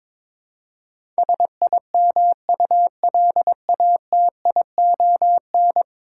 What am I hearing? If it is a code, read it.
Morse code, heard as a keyed tone, about 22 wpm, 700 Hz. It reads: SIMULATION
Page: morse-code-ninja-practice-22wpm